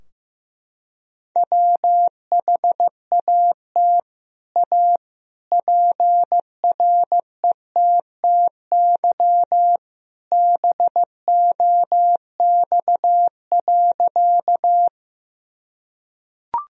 Morse code, heard as a keyed tone, about 15 wpm, 700 Hz. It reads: WHAT A PRETTY BOX.
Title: Sentences from Top 1000 Words - Review Part 2 - 15wpm